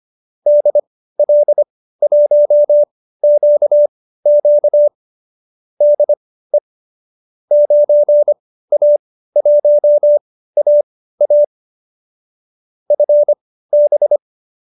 Morse code, heard as a keyed tone, about 25 wpm, spaced as for 15 wpm, 590 Hz. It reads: DL1QQ DE 9A1AA FB